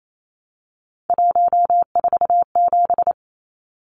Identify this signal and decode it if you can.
Morse code, heard as a keyed tone, about 28 wpm, 700 Hz. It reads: 147
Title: Numbers - 3-Digit 28wpm